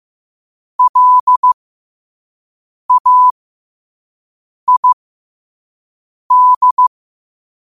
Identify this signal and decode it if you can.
Morse code, heard as a keyed tone, about 15 wpm, spaced as for 5 wpm, 1000 Hz. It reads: LAID
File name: qotdfiles / 5wpm